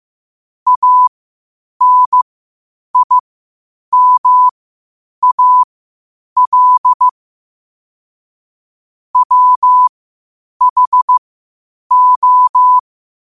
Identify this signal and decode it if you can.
Morse code, heard as a keyed tone, about 15 wpm, spaced as for 8 wpm, 1000 Hz. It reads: ANIMAL WHO
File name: qotdfiles / 8wpm